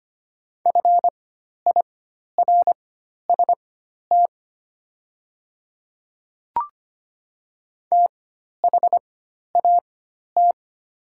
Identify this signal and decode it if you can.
Morse code, heard as a keyed tone, about 25 wpm, spaced as for 12 wpm, 700 Hz. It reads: FIRST THAT
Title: 4x Character Spacing - Top 100 Words - 25wpm